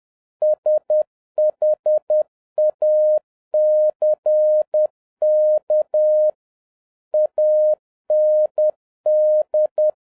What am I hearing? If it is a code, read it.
Morse code, heard as a keyed tone, about 10 wpm, 610 Hz. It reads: SHACK AND
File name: 2011-07-mp3 / n46 JOHN K3TN 10wpm